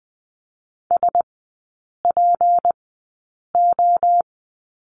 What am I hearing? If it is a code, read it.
Morse code, heard as a keyed tone, about 20 wpm, spaced as for 8 wpm, 700 Hz. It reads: SPO
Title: 152 Lesson 25e - Letter p - Three Characters 20wpm